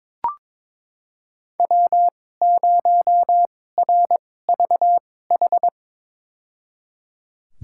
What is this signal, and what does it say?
Morse code, heard as a keyed tone, about 22 wpm, spaced as for 16 wpm, 700 Hz. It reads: W0RVH
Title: USA Call Signs - 5 Characters - 2x Character Spacing - 22wpm